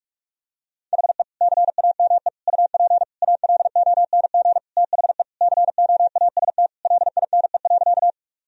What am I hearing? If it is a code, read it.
Morse code, heard as a keyed tone, about 46 wpm, 700 Hz. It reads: HE CAME UP ALONG THE COAST LINE.
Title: Sentences from Top 1000 Words - Review Part 4 - 45wpm